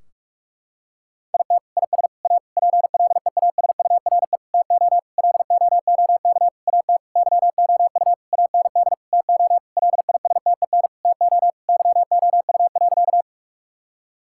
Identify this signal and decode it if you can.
Morse code, heard as a keyed tone, about 41 wpm, 700 Hz. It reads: IT IS A PLEASURE TO LOOK AT YOU AND TO LISTEN TO YOU.
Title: Sentences from Top 1600 Words 45wpm